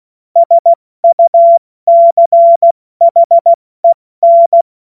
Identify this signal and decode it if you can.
Morse code, heard as a keyed tone, about 16 wpm, 680 Hz. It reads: SUCHEN
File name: Morsekurs / Witze / Witz 20. Januar 2026 70 F80